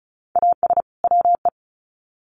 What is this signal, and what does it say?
Morse code, heard as a keyed tone, about 34 wpm, 700 Hz. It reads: AS WE
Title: Sets of 2 Words 35wpm